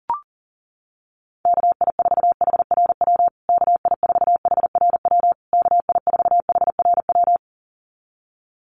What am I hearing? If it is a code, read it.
Morse code, heard as a keyed tone, about 40 wpm, 700 Hz. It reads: KI4HRW KI4HRW KI4HRW
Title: USA Call Signs Repeated 3x 40wpm